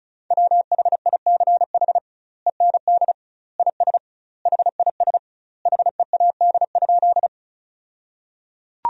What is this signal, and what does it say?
Morse code, heard as a keyed tone, about 36 wpm, 700 Hz. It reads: WHICH END IS HIS HEAD?
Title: 2x Word-Spacing - Sentences from Top 300 Words - 35wpm